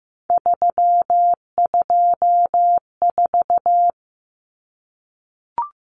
Morse code, heard as a keyed tone, about 15 wpm, 700 Hz. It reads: 324
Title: Numbers - 3-Digit 15wpm